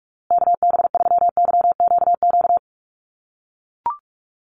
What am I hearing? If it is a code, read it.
Morse code, heard as a keyed tone, about 44 wpm, 700 Hz. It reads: KB3YQQ
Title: USA Call Signs Repeated 3x 45wpm